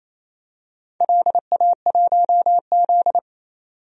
Morse code, heard as a keyed tone, about 28 wpm, 700 Hz. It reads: LA1Z
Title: DXCC-100 Call Signs - 4 Characters - 28wpm